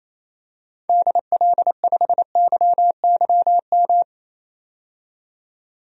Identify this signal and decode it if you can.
Morse code, heard as a keyed tone, about 28 wpm, 700 Hz. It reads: DL5YYM
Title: DXCC-100 Call Signs - 6 Characters - 28wpm